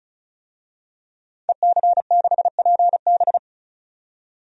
Morse code, heard as a keyed tone, about 35 wpm, 700 Hz. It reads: EC6PB
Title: DXCC-100 Call Signs - 5 Characters - 35wpm